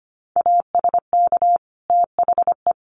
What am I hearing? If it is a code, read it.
Morse code, heard as a keyed tone, about 25 wpm, 700 Hz. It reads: ASK THE
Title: Sentences from Top 1500 Words 25wpm